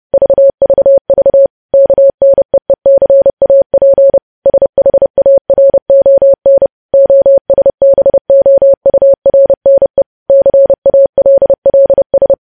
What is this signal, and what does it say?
Morse code, heard as a keyed tone, about 30 wpm, 550 Hz. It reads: VVV KNEECAP SHARON OSBOURNE CALLS
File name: podcast / News headlines-1x-30x30